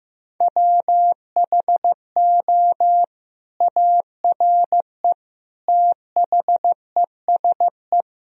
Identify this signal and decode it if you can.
Morse code, heard as a keyed tone, about 15 wpm, 700 Hz. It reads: WHO ARE THESE
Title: Sentences from Top 1000 Words - Review Part 1 - 15wpm